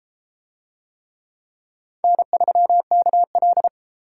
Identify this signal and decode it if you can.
Morse code, heard as a keyed tone, about 32 wpm, 700 Hz. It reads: N3KL